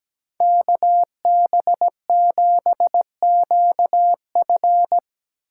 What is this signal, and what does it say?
Morse code, heard as a keyed tone, about 19 wpm, 700 Hz. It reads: KB7QF
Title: Speed Racing - USA Call Signs 17wpm